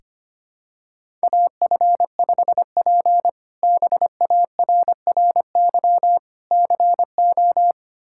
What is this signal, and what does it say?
Morse code, heard as a keyed tone, about 25 wpm, 700 Hz. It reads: AF5P BARRY CO
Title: K1USN SST QSO Practice - 25wpm